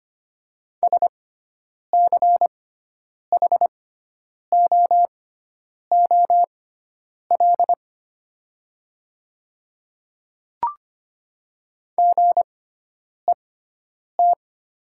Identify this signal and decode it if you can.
Morse code, heard as a keyed tone, about 25 wpm, spaced as for 9 wpm, 690 Hz. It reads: SCHOOL GET
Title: 6x Character Spacing - Top 300 Words - 25wpm